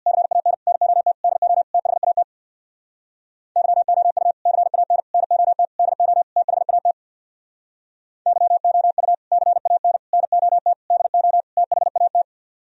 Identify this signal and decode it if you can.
Morse code, heard as a keyed tone, about 64 wpm, 700 Hz. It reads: CAN NOT DO THAT YOU CAN NOT DO THAT YOU CAN NOT DO THAT